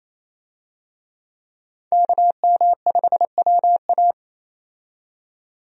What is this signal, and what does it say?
Morse code, heard as a keyed tone, about 28 wpm, 700 Hz. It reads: KM5WA
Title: USA Call Signs - 5 Characters - 28wpm